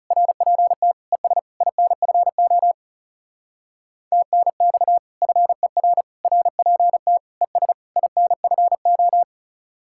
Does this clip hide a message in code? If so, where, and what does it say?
Yes: Morse code from 0.1 to 9.2 s, heard as a keyed tone, about 38 wpm, 700 Hz. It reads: RPT ES INFO TNX FER RPT ES INFO